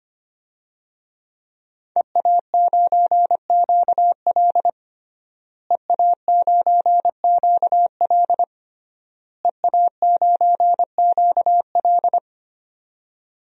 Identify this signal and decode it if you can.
Morse code, heard as a keyed tone, about 25 wpm, 700 Hz. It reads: EA9QL EA9QL EA9QL
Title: DXCC-100 Call Signs - 5 Characters - Repeated 3x - 25wpm